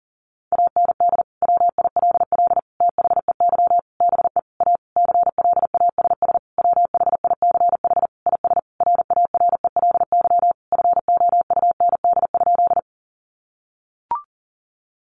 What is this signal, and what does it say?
Morse code, heard as a keyed tone, about 40 wpm, 700 Hz. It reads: AND WILL THEY BE A CLASS WHICH IS RARELY FOUND?
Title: Sentences from Top 1400 Words 40wpm